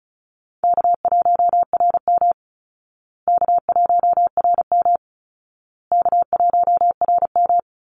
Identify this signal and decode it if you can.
Morse code, heard as a keyed tone, about 34 wpm, 700 Hz. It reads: K1RM K1RM K1RM
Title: Easy Call Signs Repeated 3x - 35wpm